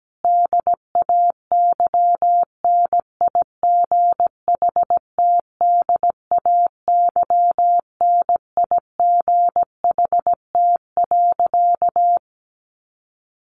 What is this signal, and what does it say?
Morse code, heard as a keyed tone, about 17 wpm, 700 Hz. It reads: DAY NIGHT DAY NIGHT.
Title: A Sound Of Thunder 17wpm